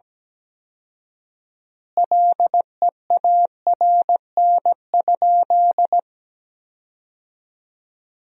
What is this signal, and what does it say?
Morse code, heard as a keyed tone, about 17 wpm, 700 Hz. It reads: LEARN?
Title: 4x Word-Spacing - Sentences from Top 300 Words - 17wpm